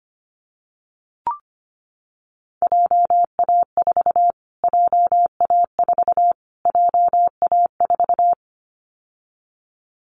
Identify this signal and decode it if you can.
Morse code, heard as a keyed tone, about 25 wpm, 700 Hz. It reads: JA4 JA4 JA4
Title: Callsign Prefixes for Top 25 DXCC - Repeated 3x - 25wpm